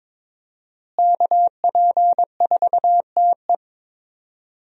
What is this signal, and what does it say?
Morse code, heard as a keyed tone, about 22 wpm, 700 Hz. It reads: KP4TE
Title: DXCC-100 Call Signs - 5 Characters - Repeated 2x - 22wpm